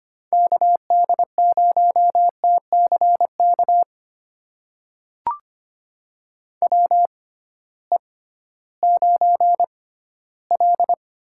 Morse code, heard as a keyed tone, about 25 wpm, spaced as for 14 wpm, 700 Hz. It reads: KD0TCK WE9L
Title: USA Call Signs - 6x Character Spacing - 25wpm